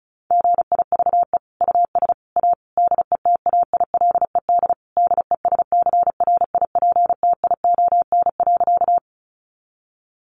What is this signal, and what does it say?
Morse code, heard as a keyed tone, about 34 wpm, 700 Hz. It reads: GIVE US A DETAILED DESCRIPTION.
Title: Sentences from Top 1500 Words 35wpm